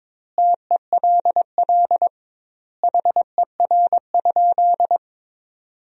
Morse code, heard as a keyed tone, about 22 wpm, 700 Hz. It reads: TELL HER?
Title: 2x Word-Spacing - Sentences from Top 200 Words - 22wpm